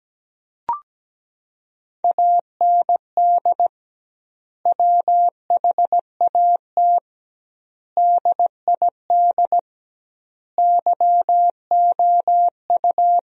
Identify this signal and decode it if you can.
Morse code, heard as a keyed tone, about 17 wpm, 700 Hz. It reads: AND WHAT DID YOU
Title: 2x Word-Spacing - Sentences from Top 300 Words - 17wpm